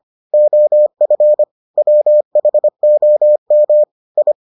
Morse code, heard as a keyed tone, about 25 wpm, 600 Hz. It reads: OF WHOM I